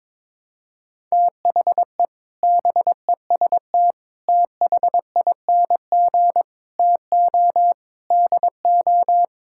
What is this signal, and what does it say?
Morse code, heard as a keyed tone, about 22 wpm, 700 Hz. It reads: THE BEST THING TO DO
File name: practice / 22wpm / Sets of 8 Words - 22wpm